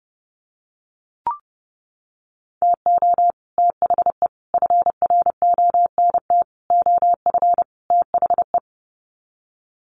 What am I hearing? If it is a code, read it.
Morse code, heard as a keyed tone, about 30 wpm, 700 Hz. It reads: TO THE FRONT OF THE
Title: Sets of 5 Words 30wpm